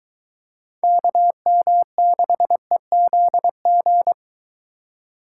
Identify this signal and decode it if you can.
Morse code, heard as a keyed tone, about 23 wpm, 700 Hz. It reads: KM6EZG